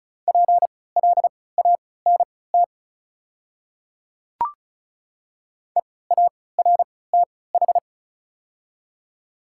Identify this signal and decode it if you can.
Morse code, heard as a keyed tone, about 36 wpm, spaced as for 20 wpm, 690 Hz. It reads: PLANT EARTH
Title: 3x Character Spacing - Top 300 Words - 35wpm